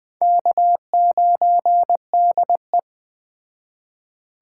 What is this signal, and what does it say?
Morse code, heard as a keyed tone, about 20 wpm, 700 Hz. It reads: K9DE